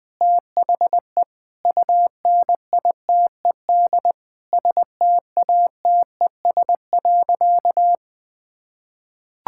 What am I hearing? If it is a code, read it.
Morse code, heard as a keyed tone, about 20 wpm, 700 Hz. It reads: THE UNITED STATES.
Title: Sentences with 2-Word Phrases - 20wpm